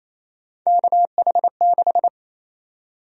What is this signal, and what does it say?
Morse code, heard as a keyed tone, about 28 wpm, 700 Hz. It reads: KH6
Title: USA Call Sign Prefixes - 28wpm